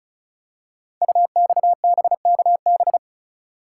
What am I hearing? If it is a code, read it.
Morse code, heard as a keyed tone, about 36 wpm, 700 Hz. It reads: UXBKB
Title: Code Groups - 5 Letters - 35wpm